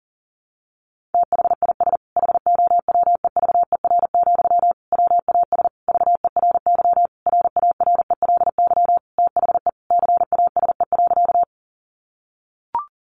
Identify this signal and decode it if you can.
Morse code, heard as a keyed tone, about 40 wpm, 700 Hz. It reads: THIS HOWEVER, WAS VERY RARELY THE CASE.
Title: Sentences from Top 1400 Words 40wpm